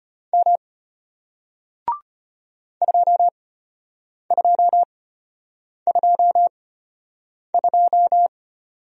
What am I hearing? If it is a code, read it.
Morse code, heard as a keyed tone, about 33 wpm, spaced as for 8 wpm, 700 Hz. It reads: M 2222